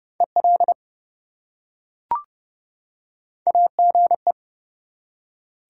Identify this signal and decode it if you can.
Morse code, heard as a keyed tone, about 30 wpm, 700 Hz. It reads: EL AGE
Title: Top 180 QSO Elements 30wpm